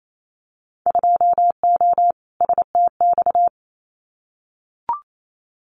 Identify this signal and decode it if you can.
Morse code, heard as a keyed tone, about 28 wpm, 700 Hz. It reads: 2O STX
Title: Winter Field Day Exchanges 28wpm